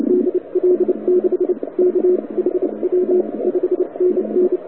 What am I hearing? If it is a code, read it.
Morse code, heard as a keyed tone, about 29 wpm, spaced as for 24 wpm, 360 Hz, with radio noise. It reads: DL6KHW 5NN
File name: wwcw08 / k5zd wwcw08 1400z